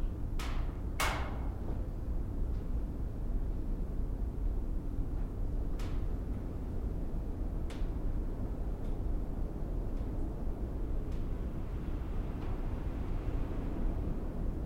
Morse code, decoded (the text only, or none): none